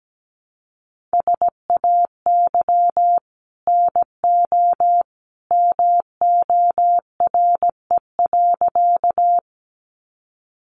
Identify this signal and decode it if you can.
Morse code, heard as a keyed tone, about 17 wpm, 700 Hz. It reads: SAY NO MORE.
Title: Sentences from Top 200 Words 17wpm